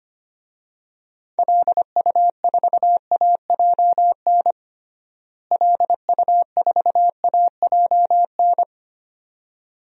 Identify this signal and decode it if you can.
Morse code, heard as a keyed tone, about 25 wpm, 700 Hz. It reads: LU4AJN LU4AJN
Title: DXCC-100 Call Signs - 6 Characters - Repeated 2x - 25wpm